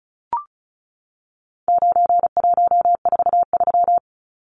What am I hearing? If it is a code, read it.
Morse code, heard as a keyed tone, about 34 wpm, 700 Hz. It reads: 9143
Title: Numbers - 4-Digit 35wpm